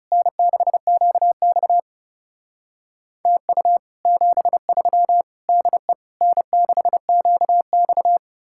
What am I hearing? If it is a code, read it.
Morse code, heard as a keyed tone, about 36 wpm, 700 Hz. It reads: N6QX TU 73 DE N6QX
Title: Speed-Racing Common QSO Phrases Encore - 30wpm